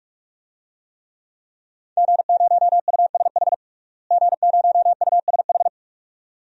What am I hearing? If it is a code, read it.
Morse code, heard as a keyed tone, about 46 wpm, 700 Hz. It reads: G0USH G0USH
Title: DXCC-100 Call Signs - 5 Characters - Repeated 2x - 45wpm